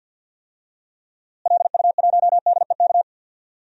Morse code, heard as a keyed tone, about 46 wpm, 700 Hz. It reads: LU1DEK
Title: DXCC-100 Call Signs - 6 Characters - Repeated 3x - 50wpm